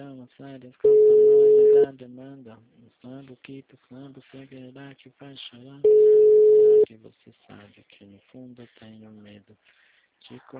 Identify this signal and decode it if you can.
Ringback tone (Europe, 425 Hz), with speech.
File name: out- -2007- - - .93773